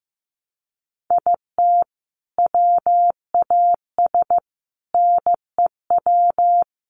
Morse code, heard as a keyed tone, about 15 wpm, 700 Hz. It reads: IT WAS NEW
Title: Sentences from Top 1100 Words 15wpm